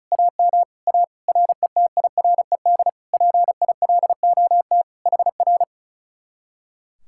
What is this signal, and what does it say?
Morse code, heard as a keyed tone, about 41 wpm, 700 Hz. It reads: AM A RETIRED PILOT HR
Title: Speed-Racing Common QSO Phrases Encore - 35wpm